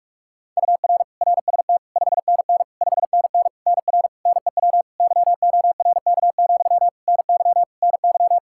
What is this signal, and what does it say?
Morse code, heard as a keyed tone, about 43 wpm, 700 Hz. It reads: UR RST 5NN 5NN NR NEW YORK, NY NY
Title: K1USN SST QSO Practice 45wpm